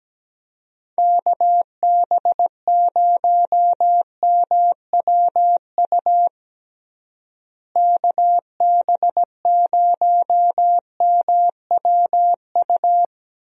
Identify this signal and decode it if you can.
Morse code, heard as a keyed tone, about 17 wpm, 700 Hz. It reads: KB0MWU KB0MWU
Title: USA Call Signs - 6 Characters - Repeated 3x - 17wpm